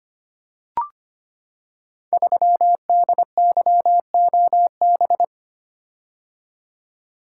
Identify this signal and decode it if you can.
Morse code, heard as a keyed tone, about 25 wpm, 700 Hz. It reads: 3DYOB
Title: Code Groups - 5 Letter:Numbers - 25wpm